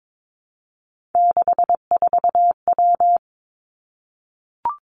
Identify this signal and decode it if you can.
Morse code, heard as a keyed tone, about 22 wpm, 700 Hz. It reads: 64W
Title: Three Challenging Letter-Number Combinations 22wpm